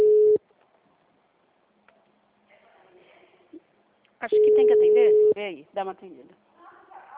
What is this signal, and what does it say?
Ringback tone (Europe, 425 Hz), with speech.